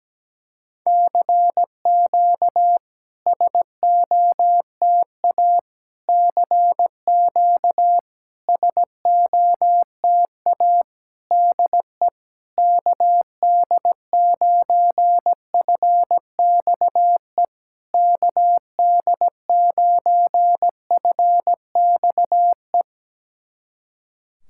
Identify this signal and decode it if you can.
Morse code, heard as a keyed tone, about 17 wpm, 690 Hz. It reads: CQ SOTA CQ SOTA DE KD9FXE KD9FXE
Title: SOTA QSO Practice 17wpm